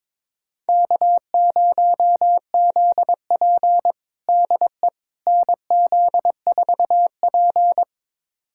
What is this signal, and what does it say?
Morse code, heard as a keyed tone, about 22 wpm, 700 Hz. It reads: K0ZP DE NZ4P
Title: USA Callsign DE Callsign - 4 Characters - 22wpm